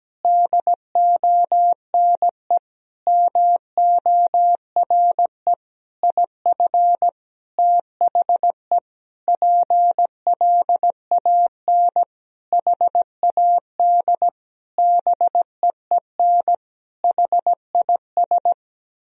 Morse code, heard as a keyed tone, about 17 wpm, 700 Hz. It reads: DONE MORE IF THE PLAN HAD BEEN HIS
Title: Sentences from Top 1000 Words - Review Part 2 - 17wpm